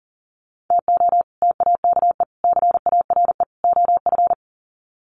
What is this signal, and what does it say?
Morse code, heard as a keyed tone, about 40 wpm, 700 Hz. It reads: TO TAKE CARE OF
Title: Sets of 4 Words 40wpm